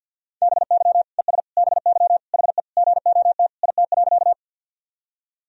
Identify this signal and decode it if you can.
Morse code, heard as a keyed tone, about 47 wpm, 700 Hz. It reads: BY ES BY HE GOT IT.
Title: Sentences from Top 300 Words and Abbreviations 50wpm